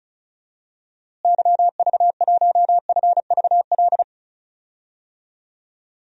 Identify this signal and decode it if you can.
Morse code, heard as a keyed tone, about 34 wpm, 700 Hz. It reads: YV1FVL
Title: DXCC-100 Call Signs - 6 Characters - Repeated 2x - 35wpm